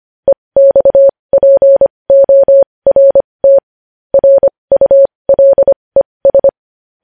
Morse code, heard as a keyed tone, about 25 wpm, spaced as for 20 wpm, 550 Hz. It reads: EXPORT RULES
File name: podcast / News headlines-1x-25x20